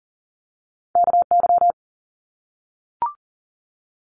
Morse code, heard as a keyed tone, about 40 wpm, 700 Hz. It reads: KY
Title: ARRL Sections 40wpm